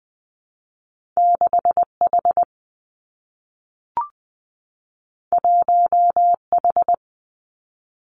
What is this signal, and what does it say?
Morse code, heard as a keyed tone, about 20 wpm, 700 Hz. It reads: 6 H 1 H
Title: Two Challenging Letter-Number Combinations 20wpm